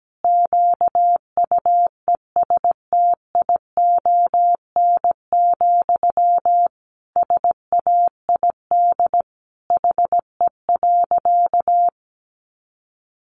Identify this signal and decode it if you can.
Morse code, heard as a keyed tone, about 17 wpm, 700 Hz. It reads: QUESTION, SAID HE.
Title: Sentences from Top 400 Words 17wpm